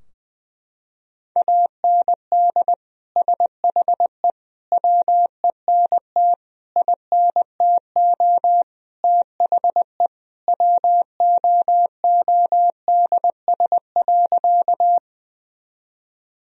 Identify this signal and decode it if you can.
Morse code, heard as a keyed tone, about 21 wpm, 700 Hz. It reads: AND SHE WENT INTO THE WOODS.
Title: Sentences from Top 1000 Words - Review Part 2 - 20wpm